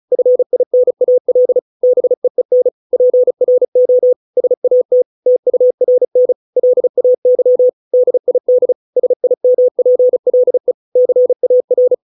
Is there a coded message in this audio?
Yes: Morse code from 0.1 to 11.9 s, heard as a keyed tone, about 34 wpm, 500 Hz. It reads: FINAL BEEN PRO SAT TURN LAY DID SIMPLE CAR